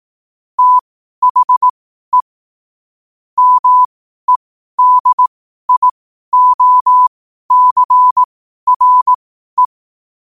Morse code, heard as a keyed tone, about 18 wpm, spaced as for 12 wpm, 1000 Hz. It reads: THE MEDIOCRE
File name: qotdfiles / 12wpm